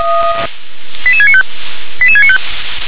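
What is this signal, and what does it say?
Five-tone selective call (ZVEI), heard as quick runs of five tones, tones of about 70 ms. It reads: 88695 88695
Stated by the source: ZVEI five-tone selective call received with an SDR.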